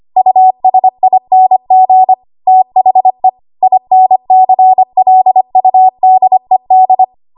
Morse code, heard as a keyed tone, about 25 wpm, 750 Hz. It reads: USING THE INCLUDED
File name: Archive / 25 WPM / 25WPM